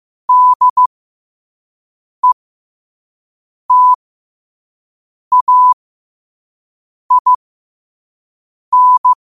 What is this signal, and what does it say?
Morse code, heard as a keyed tone, about 15 wpm, spaced as for 5 wpm, 1000 Hz. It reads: DETAIN
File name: qotdfiles / 5wpm